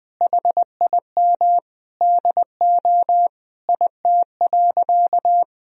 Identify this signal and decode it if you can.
Morse code, heard as a keyed tone, about 20 wpm, 700 Hz. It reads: HIM DO IT.
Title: Sentences from Top 1000 Words - Review Part 3 - 20wpm